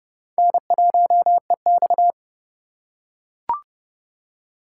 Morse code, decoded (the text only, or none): N1EX